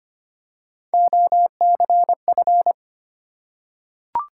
Morse code, heard as a keyed tone, about 25 wpm, 700 Hz. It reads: OCF
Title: Top 180 QSO Elements 25wpm